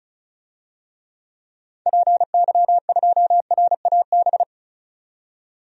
Morse code, heard as a keyed tone, about 36 wpm, 700 Hz. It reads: PY2RAB
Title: DXCC-100 Call Signs - 6 Characters - 35wpm